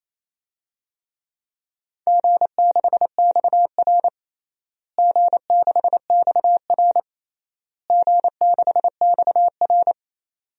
Morse code, heard as a keyed tone, about 28 wpm, 700 Hz. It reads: G6XR G6XR G6XR
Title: DXCC-100 Call Signs - 4 Characters - Repeated 3x - 28wpm